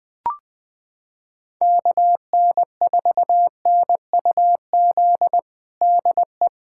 Morse code, heard as a keyed tone, about 21 wpm, 700 Hz. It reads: KN4NUZ DE
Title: USA Callsign DE Callsign 20wpm